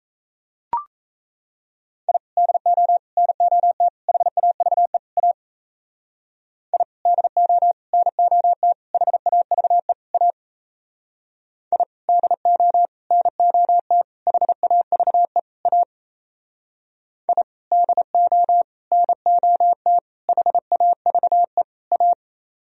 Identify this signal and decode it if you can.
Morse code, heard as a keyed tone, about 34 wpm, 700 Hz. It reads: I DO NOT HAVE A I DO NOT HAVE A I DO NOT HAVE A I DO NOT HAVE A
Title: Speed-Racing - Sets of 5 Words 28wpm